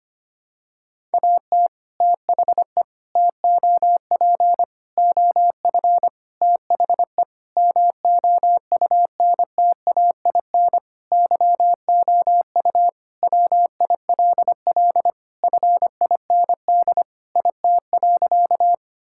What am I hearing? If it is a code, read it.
Morse code, heard as a keyed tone, about 25 wpm, 700 Hz. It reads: AT THE TOP OF THE MOUNTAIN YOU WILL FIND IT.
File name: practice / 25wpm / Sentences with 3-Word Phrases - 25wpm